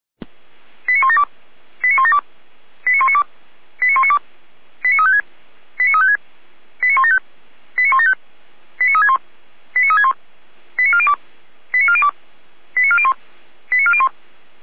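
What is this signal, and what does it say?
Five-tone selective call (ZVEI), heard as quick runs of five tones, tones of about 70 ms. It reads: 89172 89172 89192 89192 89356 89356 89175 89175 89361 89361 89402 89402 89401 89401
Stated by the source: ZVEI five-tone selective call received by SDR.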